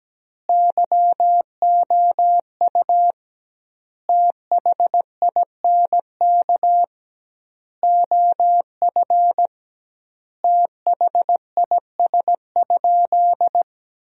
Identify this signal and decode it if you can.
Morse code, heard as a keyed tone, about 17 wpm, 700 Hz. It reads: YOU THINK OF THIS?
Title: 2x Word-Spacing - Sentences from Top 200 Words - 17wpm